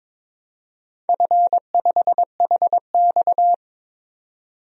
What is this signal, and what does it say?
Morse code, heard as a keyed tone, about 22 wpm, 700 Hz. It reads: F5HX
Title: DXCC-100 Call Signs - 4 Characters - 22wpm